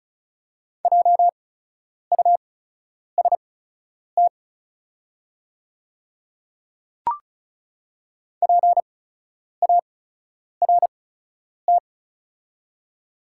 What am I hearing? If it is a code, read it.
Morse code, heard as a keyed tone, about 36 wpm, spaced as for 10 wpm, 700 Hz. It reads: JUST PART
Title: 8x Character Spacing - Top 300 Words - 35wpm